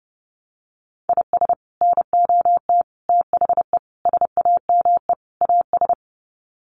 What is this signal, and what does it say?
Morse code, heard as a keyed tone, about 30 wpm, 700 Hz. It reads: IS NOT THE SAME AS
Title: Sets of 5 Words Encore 30wpm